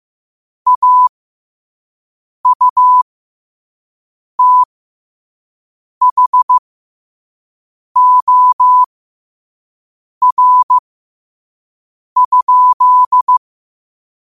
Morse code, heard as a keyed tone, about 15 wpm, spaced as for 5 wpm, 1000 Hz. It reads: AUTHOR?
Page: Quotes for Wed, 13 Aug 2025 in Morse Code at 5 words per minute.